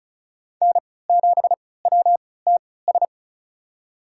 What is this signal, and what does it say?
Morse code, heard as a keyed tone, about 34 wpm, spaced as for 20 wpm, 700 Hz. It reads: N7WTS
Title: USA Call Signs - 5 Characters - 3x Character Spacing - 35wpm